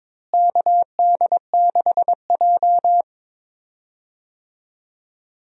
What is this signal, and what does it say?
Morse code, heard as a keyed tone, about 22 wpm, 700 Hz. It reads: KD6J
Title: USA Call Signs - 4 Characters - Repeated 3x - 22wpm